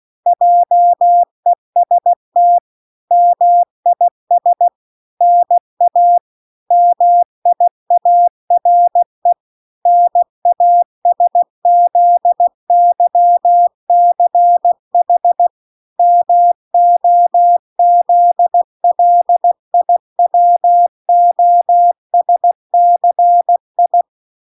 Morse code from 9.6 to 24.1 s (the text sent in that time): NASZYCH MOZLIWOSCI